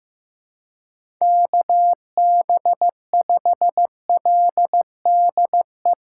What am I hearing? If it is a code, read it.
Morse code, heard as a keyed tone, about 15 wpm, 700 Hz. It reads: KB5LDE
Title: USA Call Signs - 8x Character Spacing - 15wpm